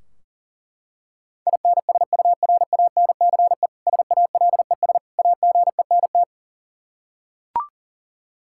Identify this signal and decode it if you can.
Morse code, heard as a keyed tone, about 40 wpm, 700 Hz. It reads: INSURANCE SALES AGENT
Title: Practice QSO - Exchange 3 - More Info - 40wpm